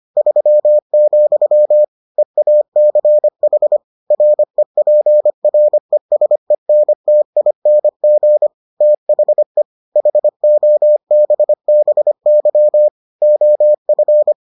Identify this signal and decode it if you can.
Morse code, heard as a keyed tone, about 25 wpm, 600 Hz. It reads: 3, EACH REPRESENTING THE HOBBY OF